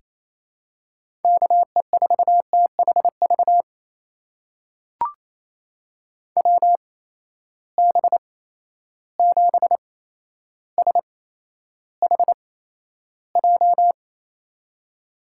Morse code, heard as a keyed tone, about 28 wpm, spaced as for 12 wpm, 700 Hz. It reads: KE4THV WB7SHJ
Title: USA Call Signs - 6 Characters - 8x Character Spacing - 28wpm